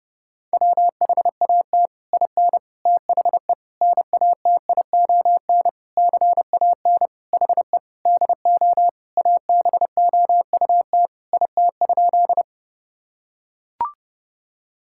Morse code, heard as a keyed tone, about 32 wpm, 700 Hz. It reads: WHAT IN THE NATION CAN HE DO ABOUT IT?
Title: Sentences from Top 1000 Words - Review Part 6 - 30wpm